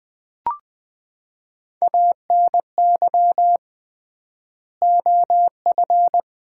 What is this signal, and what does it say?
Morse code, heard as a keyed tone, about 20 wpm, 700 Hz. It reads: ANY OF
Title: 3x Word-Spacing - Sets of 3 Words - 20wpm